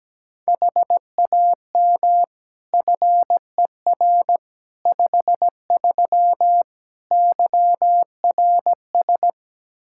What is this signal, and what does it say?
Morse code, heard as a keyed tone, about 17 wpm, 700 Hz. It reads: HAM FER 53 YRS
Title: Practice QSO - Exchange 3 - More Info - 17wpm